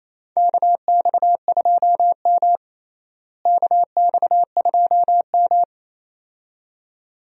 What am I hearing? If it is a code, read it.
Morse code, heard as a keyed tone, about 28 wpm, 700 Hz. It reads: KX2M KX2M
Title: USA Call Signs - 4 Characters - Repeated 2x - 28wpm